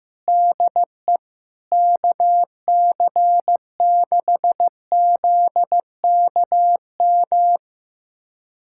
Morse code, heard as a keyed tone, about 16 wpm, 700 Hz. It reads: DE KC6ZKM
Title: USA Callsign DE Callsign - 6 Characters - 15wpm